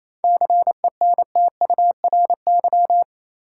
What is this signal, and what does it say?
Morse code, heard as a keyed tone, about 28 wpm, 700 Hz. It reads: CENTURY
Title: Sets of 5 Words - New Word Drill - 28wpm